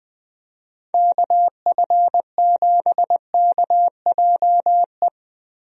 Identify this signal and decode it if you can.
Morse code, heard as a keyed tone, about 20 wpm, 700 Hz. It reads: KF7KJE
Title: USA Call Signs - 6 Characters - Repeated 2x - 20wpm